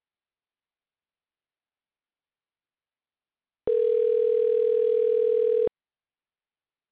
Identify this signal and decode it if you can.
Ringback tone (North America), 440 + 480 Hz.